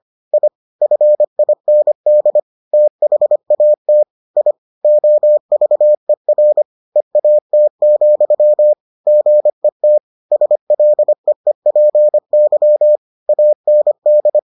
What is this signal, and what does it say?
Morse code, heard as a keyed tone, about 25 wpm, 600 Hz. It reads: I FIND THAT I OVER EAT, GET SLEEPY AND